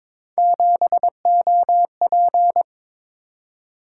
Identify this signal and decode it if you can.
Morse code, heard as a keyed tone, about 22 wpm, 700 Hz. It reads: 7OP
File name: practice / 22wpm / Three Letter-Number Combinations - 22wpm